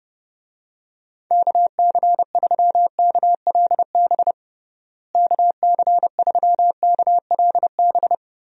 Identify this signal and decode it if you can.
Morse code, heard as a keyed tone, about 28 wpm, 700 Hz. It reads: KC3KLB KC3KLB
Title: USA Call Signs - 6 Characters - Repeated 3x - 30wpm